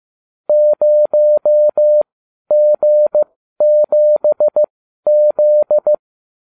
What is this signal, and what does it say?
Morse code, heard as a keyed tone, about 15 wpm, spaced as for 12 wpm, 600 Hz. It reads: 0G7Z